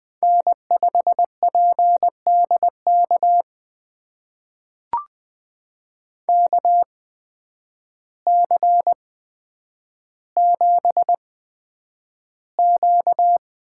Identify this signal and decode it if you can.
Morse code, heard as a keyed tone, about 20 wpm, spaced as for 9 wpm, 700 Hz. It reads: N5PDK KC7Q
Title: USA Call Signs - 5 Characters - 8x Character Spacing - 20wpm